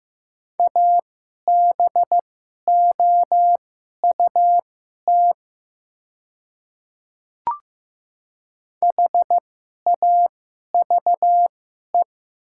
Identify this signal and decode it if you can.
Morse code, heard as a keyed tone, about 15 wpm, spaced as for 11 wpm, 700 Hz. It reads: ABOUT HAVE
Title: 2x Character Spacing - Top 200 Words - 15wpm